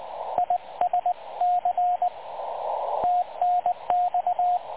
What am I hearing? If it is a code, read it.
Morse code, heard as a keyed tone, about 20 wpm, 720 Hz, with radio noise. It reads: ISC TNX
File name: dlazaawansowanych / nagraniazpasma / websdr recording 3567.0kHz